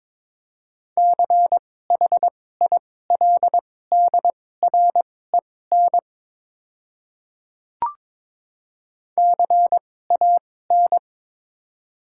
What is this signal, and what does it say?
Morse code, heard as a keyed tone, about 21 wpm, spaced as for 16 wpm, 700 Hz. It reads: CHILDREN CAN
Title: 2x Character Spacing - Top 300 Words - 22wpm